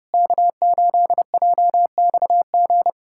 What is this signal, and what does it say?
Morse code, heard as a keyed tone, about 32 wpm, 700 Hz. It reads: K8JXG
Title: USA Callsign DE Callsign 30wpm